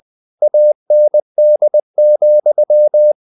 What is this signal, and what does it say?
Morse code, heard as a keyed tone, about 20 wpm, 600 Hz. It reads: AND,
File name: the old man and the sea / the old man and the sea part 1 20wpm